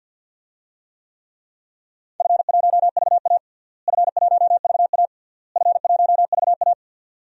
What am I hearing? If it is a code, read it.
Morse code, heard as a keyed tone, about 52 wpm, 700 Hz. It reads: F1VA F1VA F1VA
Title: DXCC-100 Call Signs - 4 Characters - Repeated 3x - 50wpm